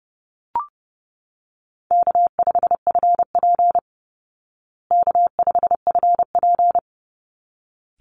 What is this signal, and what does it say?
Morse code, heard as a keyed tone, about 30 wpm, 700 Hz. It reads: K5FP K5FP
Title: Easy Call Signs Repeated 2x - 30wpm